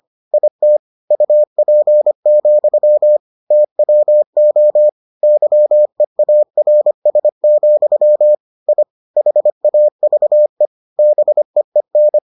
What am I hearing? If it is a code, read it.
Morse code, heard as a keyed tone, about 25 wpm, 600 Hz. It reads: IT UP, TWO YEARS, I HAVE BEEN